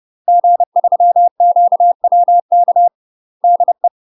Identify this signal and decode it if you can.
Morse code, heard as a keyed tone, about 30 wpm, 700 Hz. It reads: G3QWK DE
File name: UK MASTER QSOs TRAINING / 30WPM / UK MASTER QSOs 0015